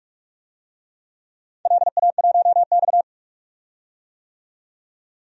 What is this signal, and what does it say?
Morse code, heard as a keyed tone, about 44 wpm, 690 Hz. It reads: LA1X
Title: DXCC-100 Call Signs - 4 Characters - 45wpm